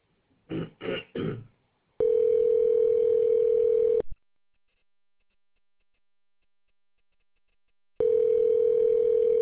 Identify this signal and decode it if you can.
Ringback tone (North America), 440 + 480 Hz.